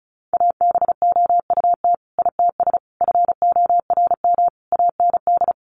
Morse code, heard as a keyed tone, about 34 wpm, 700 Hz. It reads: ABOUT ITS FORM AND
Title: Sentences from Top 1200 Words 35wpm